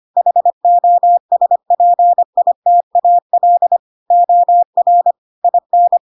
Morse code, heard as a keyed tone, about 25 wpm, 700 Hz. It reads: HOSPITAL OR IN
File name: Pinocchio / Pinocchio ch4 25 25 ik1hge